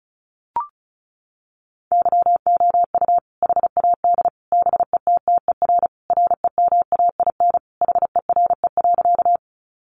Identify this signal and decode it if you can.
Morse code, heard as a keyed tone, about 34 wpm, 700 Hz. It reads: YOU HAD BETTER REMAIN HERE.
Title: Sentences from Top 1200 Words 35wpm